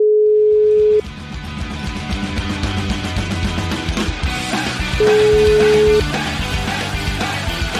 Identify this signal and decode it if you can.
Ringback tone (Europe, 425 Hz), with noise.